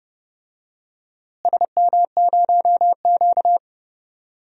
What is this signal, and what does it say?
Morse code, heard as a keyed tone, about 32 wpm, 700 Hz. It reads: SM0Q
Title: DXCC-100 Call Signs - 4 Characters - Repeated 2x - 30wpm